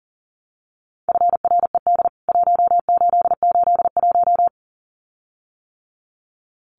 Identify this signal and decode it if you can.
Morse code, heard as a keyed tone, about 40 wpm, 700 Hz. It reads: FRED 1881
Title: CWT Exchange 40wpm